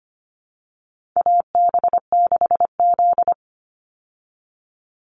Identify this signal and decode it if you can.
Morse code, heard as a keyed tone, about 25 wpm, 700 Hz. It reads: AB6Z
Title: Call Signs Repeated 2x - 25wpm